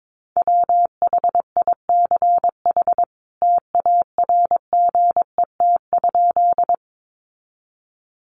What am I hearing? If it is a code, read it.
Morse code, heard as a keyed tone, about 22 wpm, 700 Hz. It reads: WHICH TARGET?
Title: Sentences from Top 1200 Words 22wpm